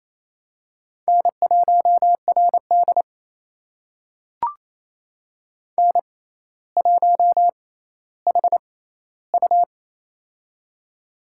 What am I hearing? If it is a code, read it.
Morse code, heard as a keyed tone, about 28 wpm, spaced as for 14 wpm, 700 Hz. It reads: N1RD N1HU
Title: USA Call Signs - 6x Character Spacing - 28wpm